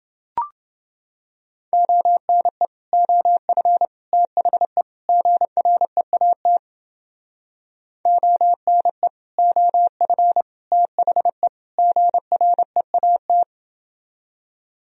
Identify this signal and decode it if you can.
Morse code, heard as a keyed tone, about 30 wpm, 700 Hz. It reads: ONE OF THE GREAT ONE OF THE GREAT